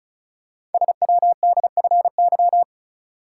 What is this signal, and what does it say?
Morse code, heard as a keyed tone, about 36 wpm, 700 Hz. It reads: SWDFY
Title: Code Groups - 5 Letters - 35wpm